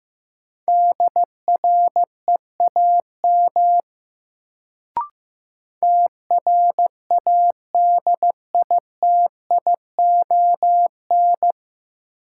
Morse code, heard as a keyed tone, about 15 wpm, 700 Hz. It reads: DREAM TRADITION
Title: Sentences from Top 1500 Words - New Word Drill - 15wpm